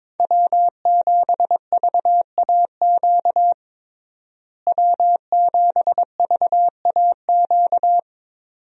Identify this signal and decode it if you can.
Morse code, heard as a keyed tone, about 21 wpm, 700 Hz. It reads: W7VAQ W7VAQ
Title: USA Call Signs - 5 Characters - Repeated 2x - 22wpm